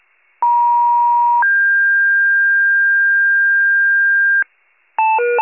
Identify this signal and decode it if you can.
Two-tone page, heard as a one-second tone, then a three-second tone, about 950 then 1600 Hz, with noise. It is heard from 0.4 to 4.4 s.